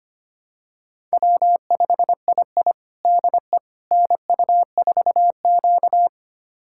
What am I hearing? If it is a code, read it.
Morse code, heard as a keyed tone, about 25 wpm, 700 Hz. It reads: W5II DE NU4Q
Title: USA Callsign DE Callsign - 4 Characters - 25wpm